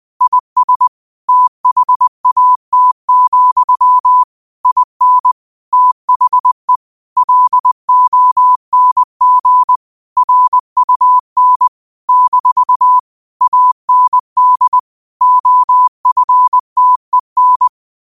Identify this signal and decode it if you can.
Morse code, heard as a keyed tone, about 20 wpm, 1000 Hz. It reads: IS THAT, IN THE LONG RUN - AND OFTEN
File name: qotdfiles / 20wpm